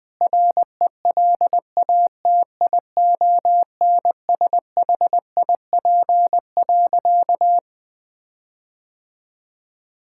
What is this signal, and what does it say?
Morse code, heard as a keyed tone, about 21 wpm, 700 Hz. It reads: RELATIONSHIP.
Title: Sentences with 6-Word Phrases - 20wpm